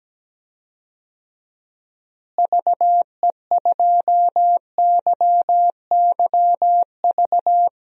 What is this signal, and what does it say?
Morse code, heard as a keyed tone, about 17 wpm, 700 Hz. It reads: VE2YYV
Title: DXCC-100 Call Signs - 6 Characters - Repeated 2x - 17wpm